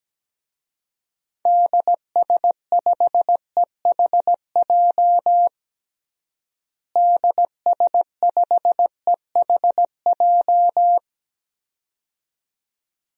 Morse code, heard as a keyed tone, about 17 wpm, 700 Hz. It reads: DS5EHJ DS5EHJ
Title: DXCC-100 Call Signs - 6 Characters - Repeated 2x - 17wpm